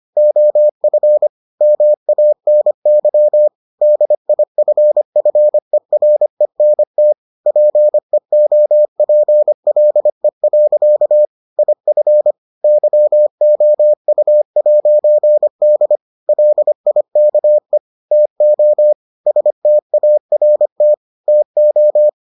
Morse code, heard as a keyed tone, about 25 wpm, 600 Hz. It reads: OF MANY DIFFERENT PEOPLE. IF YOU'D LIKE TO START TO